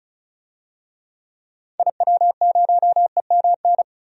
Morse code, heard as a keyed tone, about 36 wpm, 700 Hz. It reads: IW0EMN
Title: DXCC-100 Call Signs - 6 Characters - 35wpm